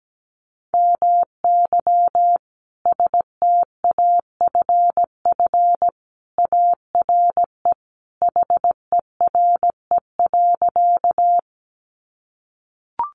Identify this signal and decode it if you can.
Morse code, heard as a keyed tone, about 17 wpm, 700 Hz. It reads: MY STAFF ARE HERE.
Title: Sentences from Top 1200 Words 17wpm